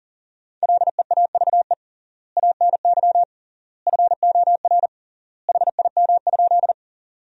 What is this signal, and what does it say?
Morse code, heard as a keyed tone, about 40 wpm, 700 Hz. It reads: LEAVE ANY FOR HIM?
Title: 3x Word-Spacing - Sentences from Top 400 Words - 40wpm